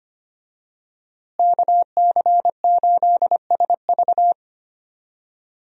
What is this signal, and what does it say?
Morse code, heard as a keyed tone, about 25 wpm, 700 Hz. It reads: KC8SV
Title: USA Call Signs - 5 Characters - Repeated 3x - 25wpm